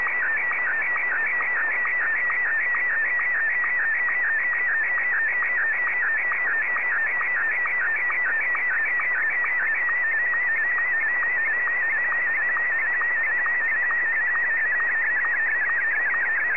SSTV (slow-scan television), Martin 1.
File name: sstv / M1-ZL2TJM